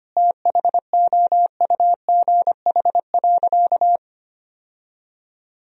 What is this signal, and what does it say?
Morse code, heard as a keyed tone, about 25 wpm, 700 Hz. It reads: THOUGH.
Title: Sentences from Top 1000 Words - Review Part 2 - 25wpm